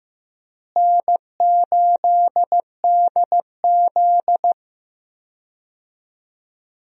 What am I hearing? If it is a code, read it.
Morse code, heard as a keyed tone, about 15 wpm, 700 Hz. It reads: N8DZ
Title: USA Call Signs - 4 Characters - Repeated 3x - 15wpm